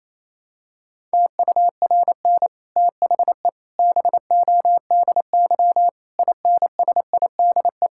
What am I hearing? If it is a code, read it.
Morse code, heard as a keyed tone, about 27 wpm, 700 Hz. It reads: TURN THE BODY INSIDE
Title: Sentences from Top 1000 Words - Review Part 6 - 28wpm